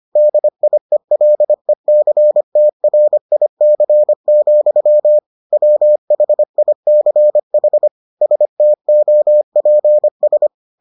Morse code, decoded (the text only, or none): DIELECTRIC, WHICH STOPS